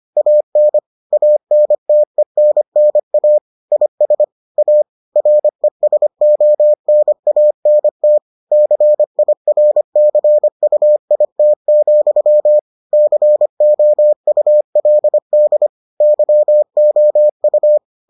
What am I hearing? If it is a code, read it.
Morse code, heard as a keyed tone, about 25 wpm, 600 Hz. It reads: AN ANTENNA IS A RESONANT CIRCUIT, COULD YOU